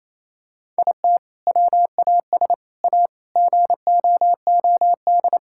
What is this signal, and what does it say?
Morse code, heard as a keyed tone, about 27 wpm, 700 Hz. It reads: IT WAS A GOOD
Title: Sentences from Top 1000 Words - Review Part 5 - 28wpm